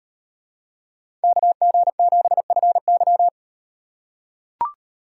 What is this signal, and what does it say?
Morse code, heard as a keyed tone, about 38 wpm, 700 Hz. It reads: KG7FY